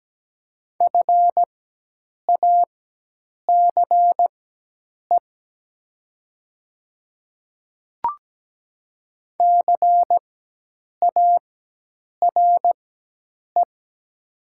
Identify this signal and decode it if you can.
Morse code, heard as a keyed tone, about 17 wpm, spaced as for 8 wpm, 700 Hz. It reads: FACE CARE
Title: 4x Character Spacing - Top 300 Words - 17wpm